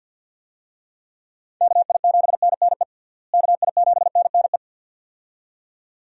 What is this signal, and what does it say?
Morse code, heard as a keyed tone, about 52 wpm, 700 Hz. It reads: KI7NNE KI7NNE
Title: USA Call Signs - 6 Characters - Repeated 2x - 50wpm